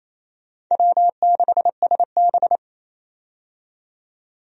Morse code, heard as a keyed tone, about 28 wpm, 700 Hz. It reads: W6SB
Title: USA Call Signs - 4 Characters - 28wpm